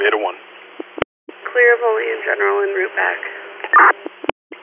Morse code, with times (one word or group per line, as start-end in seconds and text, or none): none